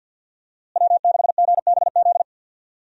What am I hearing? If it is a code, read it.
Morse code, heard as a keyed tone, about 52 wpm, 700 Hz. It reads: W6GBZ
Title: USA Call Signs - 5 Characters - Repeated 2x - 50wpm